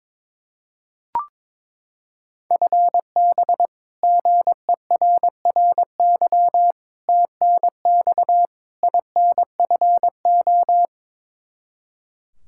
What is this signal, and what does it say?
Morse code, heard as a keyed tone, about 22 wpm, 700 Hz. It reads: FB GERRY TNX INFO
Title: Practice QSO - Exchange 3 - More Info - 22wpm